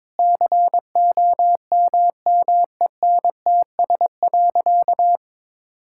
Morse code, heard as a keyed tone, about 21 wpm, 700 Hz. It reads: COMMENTS.
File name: practice / 22wpm / Sentences with 8-Word Phrases - 22wpm